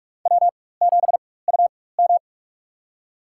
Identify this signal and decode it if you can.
Morse code, heard as a keyed tone, about 45 wpm, spaced as for 21 wpm, 700 Hz. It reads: W7UM
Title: USA Call Signs - 4x Character Spacing - 45wpm